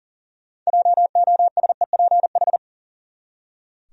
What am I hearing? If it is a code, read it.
Morse code, heard as a keyed tone, about 40 wpm, 700 Hz. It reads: JOSEPH
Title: USA Common Names - 40wpm